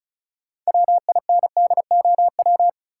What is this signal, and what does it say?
Morse code, heard as a keyed tone, about 36 wpm, 700 Hz. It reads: WINDOW
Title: Sentences from Top 1600 Words - New Word Drill Alt - 35wpm